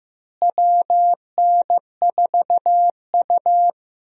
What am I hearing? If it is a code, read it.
Morse code, heard as a keyed tone, about 15 wpm, 700 Hz. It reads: WN4U
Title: USA Call Signs - 4 Characters - Repeated 2x - 15wpm